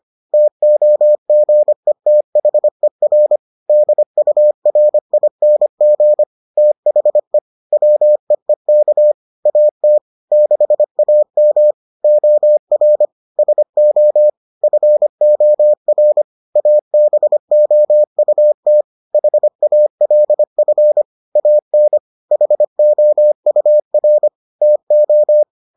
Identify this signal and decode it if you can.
Morse code, heard as a keyed tone, about 25 wpm, 600 Hz. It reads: TOGETHER DURING THE WEEK AT 6AM OR SO FOR ABOUT HALF AN HOUR TO